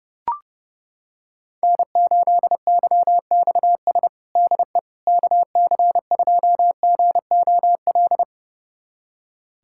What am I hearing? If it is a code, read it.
Morse code, heard as a keyed tone, about 30 wpm, 700 Hz. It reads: N8YXS DE KC2GOL
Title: USA Callsign DE Callsign 30wpm